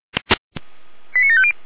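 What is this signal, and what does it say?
Five-tone selective call (ZVEI), heard as quick runs of five tones, tones of about 70 ms. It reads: 89750
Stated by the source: Five-tone ZVEI selective call, code 89584, received by SDR.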